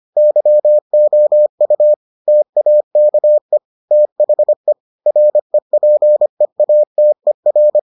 Morse code, heard as a keyed tone, about 25 wpm, 600 Hz. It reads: YOU TAKE THE REPEATER